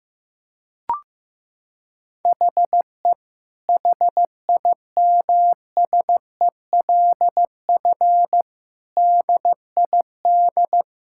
Morse code, heard as a keyed tone, about 15 wpm, 700 Hz. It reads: HE HIMSELF DID
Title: Sentences from Top 1600 Words 15wpm